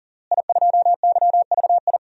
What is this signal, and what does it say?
Morse code, heard as a keyed tone, about 40 wpm, 700 Hz. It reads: I2YVI
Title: DXCC-100 Call Signs - 5 Characters - Repeated 3x - 40wpm